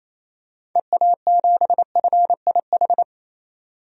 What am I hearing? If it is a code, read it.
Morse code, heard as a keyed tone, about 28 wpm, 700 Hz. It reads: EA7FIH
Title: DXCC-100 Call Signs - 6 Characters - 28wpm